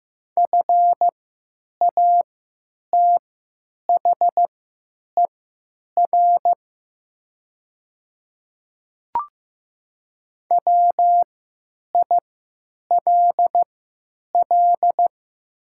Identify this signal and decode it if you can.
Morse code, heard as a keyed tone, about 15 wpm, spaced as for 9 wpm, 700 Hz. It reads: FATHER WILL
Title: 3x Character Spacing - Top 300 Words - 15wpm